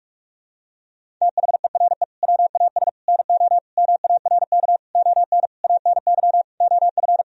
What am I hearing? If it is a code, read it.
Morse code, heard as a keyed tone, about 46 wpm, 700 Hz. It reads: THERE WAS NO MARK ON ANY OF
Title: Sentences from Top 1000 Words - Review Part 1 - 45wpm